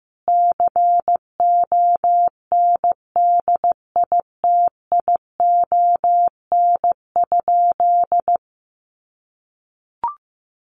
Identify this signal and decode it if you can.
Morse code, heard as a keyed tone, about 15 wpm, 700 Hz. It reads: CONDITION?
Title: Sentences from Top 1300 Words 15wpm